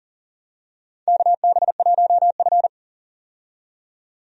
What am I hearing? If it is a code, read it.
Morse code, heard as a keyed tone, about 39 wpm, 700 Hz. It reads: KB1F
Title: USA Call Signs - 4 Characters - 40wpm